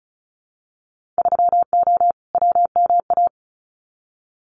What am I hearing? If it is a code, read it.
Morse code, heard as a keyed tone, about 34 wpm, 700 Hz. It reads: 3O WMA
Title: Winter Field Day Exchanges Encore 35wpm